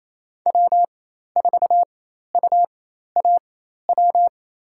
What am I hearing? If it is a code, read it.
Morse code, heard as a keyed tone, about 28 wpm, spaced as for 13 wpm, 700 Hz. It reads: W4UAW
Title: USA Call Signs - 4x Character Spacing - 28wpm